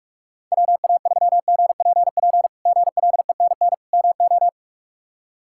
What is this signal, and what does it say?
Morse code, heard as a keyed tone, about 48 wpm, 700 Hz. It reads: WA3GPP GLENN MO
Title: K1USN SST QSO Practice - 45wpm